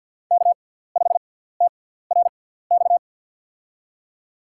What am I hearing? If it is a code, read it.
Morse code, heard as a keyed tone, about 53 wpm, spaced as for 17 wpm, 700 Hz. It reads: K5TRX
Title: USA Call Signs - 5 Characters - 6x Character Spacing - 50wpm